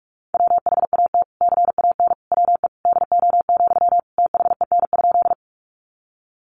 Morse code, heard as a keyed tone, about 44 wpm, 700 Hz. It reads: WHAT CAN WE DO, THEN?
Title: Sentences from Top 100 Words 45wpm